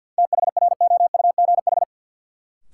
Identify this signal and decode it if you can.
Morse code, heard as a keyed tone, about 52 wpm, 700 Hz. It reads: THROUGH
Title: Sentences from Top 200 Words - New Word Drill - 50wpm